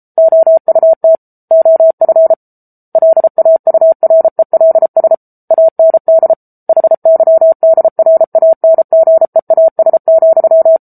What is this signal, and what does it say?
Morse code, heard as a keyed tone, about 34 wpm, 650 Hz. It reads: OUT OF LAURELS AND HYDRANGEAS,
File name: Book Two / 07-Chapter 7 - 33 WPM